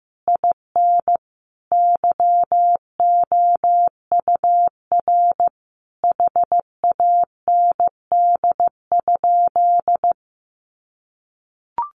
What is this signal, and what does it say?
Morse code, heard as a keyed tone, about 15 wpm, 700 Hz. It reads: IN YOUR HAND?
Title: Sentences from Top 200 Words 15wpm